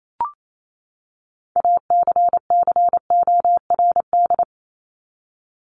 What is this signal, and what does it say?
Morse code, heard as a keyed tone, about 28 wpm, 700 Hz. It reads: ACCORD
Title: ING Suffix - Top 100 Words with Base Words 28wpm